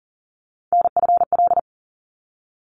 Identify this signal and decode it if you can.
Morse code, heard as a keyed tone, about 40 wpm, 700 Hz. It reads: NFL
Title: ARRL Sections 40wpm